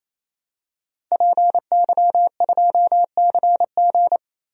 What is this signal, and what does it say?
Morse code, heard as a keyed tone, about 28 wpm, 700 Hz. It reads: PY2CG
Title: DXCC-100 Call Signs - 5 Characters - 28wpm